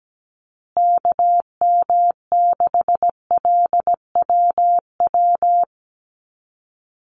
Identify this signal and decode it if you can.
Morse code, heard as a keyed tone, about 17 wpm, 700 Hz. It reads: KM6LWW
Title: USA Call Signs Repeated 2x 17wpm